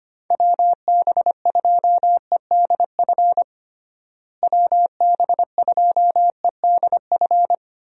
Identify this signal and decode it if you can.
Morse code, heard as a keyed tone, about 25 wpm, 700 Hz. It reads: WB2EDF WB2EDF
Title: USA Call Signs - 6 Characters - Repeated 2x - 25wpm